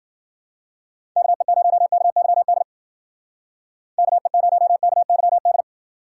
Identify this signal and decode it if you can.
Morse code, heard as a keyed tone, about 55 wpm, 700 Hz. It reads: KE0KYD KE0KYD